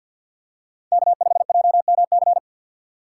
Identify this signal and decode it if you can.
Morse code, heard as a keyed tone, about 46 wpm, 700 Hz. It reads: K5JMC
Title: USA Call Signs - 5 Characters - 3x Character Spacing - 50wpm